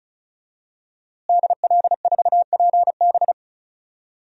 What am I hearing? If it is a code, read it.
Morse code, heard as a keyed tone, about 34 wpm, 700 Hz. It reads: DL4PB
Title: DXCC-100 Call Signs - 5 Characters - Repeated 2x - 35wpm